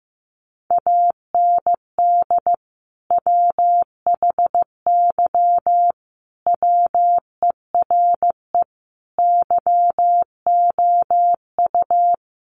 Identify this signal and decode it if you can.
Morse code, heard as a keyed tone, about 15 wpm, 700 Hz. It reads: AND WHY WERE YOU
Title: Sentences from Top 1000 Words 15wpm